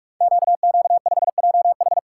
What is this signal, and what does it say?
Morse code, heard as a keyed tone, about 44 wpm, 690 Hz. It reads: QQ5JH
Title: Code Groups - 5 Letter:Numbers - 45wpm